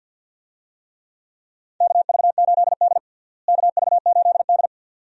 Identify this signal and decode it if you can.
Morse code, heard as a keyed tone, about 52 wpm, 700 Hz. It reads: KV8D KV8D
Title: USA Call Signs - 4 Characters - Repeated 2x - 50wpm